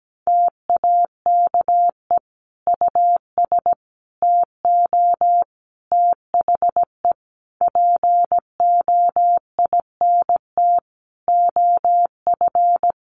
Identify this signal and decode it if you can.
Morse code, heard as a keyed tone, about 17 wpm, 700 Hz. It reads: TAKE US TO THE POINT OF
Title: Sentences from Top 1500 Words 17wpm